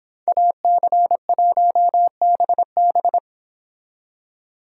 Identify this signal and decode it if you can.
Morse code, heard as a keyed tone, about 26 wpm, 700 Hz. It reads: AC1BB